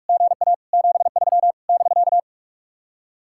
Morse code, heard as a keyed tone, about 60 wpm, spaced as for 52 wpm, 700 Hz. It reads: GA 73 <BK>
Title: Speed-Racing Common QSO Phrases - 45wpm